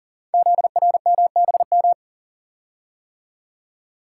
Morse code, heard as a keyed tone, about 43 wpm, 700 Hz. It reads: ZRMBM
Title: Code Groups - 5 Letters - 40wpm